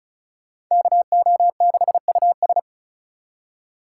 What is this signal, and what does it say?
Morse code, heard as a keyed tone, about 36 wpm, 700 Hz. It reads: KO6US
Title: USA Call Signs - 5 Characters - 2x Character Spacing - 35wpm